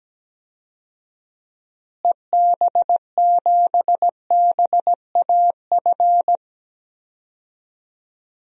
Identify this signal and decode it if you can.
Morse code, heard as a keyed tone, about 17 wpm, 700 Hz. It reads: EB7BAF
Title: DXCC-100 Call Signs - 6 Characters - Repeated 3x - 17wpm